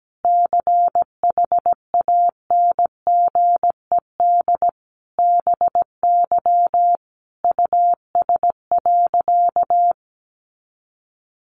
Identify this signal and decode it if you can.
Morse code, heard as a keyed tone, about 17 wpm, 700 Hz. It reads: CHANGED BY US.
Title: A Sound Of Thunder 17wpm